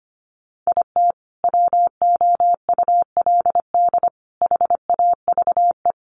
Morse code, heard as a keyed tone, about 25 wpm, 700 Hz. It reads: IT WOULD HAVE
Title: Sentences from Top 1100 Words 25wpm